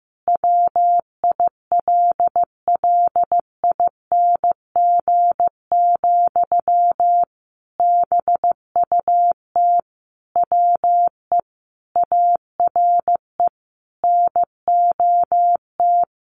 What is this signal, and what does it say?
Morse code, heard as a keyed tone, about 15 wpm, 700 Hz. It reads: WILLING, BUT WE ARE NOT
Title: Sentences from Top 1300 Words 15wpm